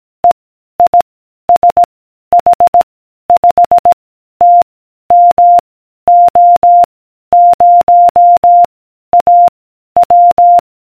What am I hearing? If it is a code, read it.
Morse code, heard as a keyed tone, about 18 wpm, 700 Hz. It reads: E I S H 5 T M O 0 A W